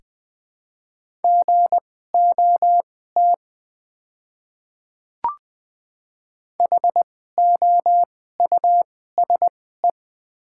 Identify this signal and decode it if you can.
Morse code, heard as a keyed tone, about 20 wpm, spaced as for 14 wpm, 700 Hz. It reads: GOT HOUSE
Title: 2x Character Spacing - Top 300 Words - 20wpm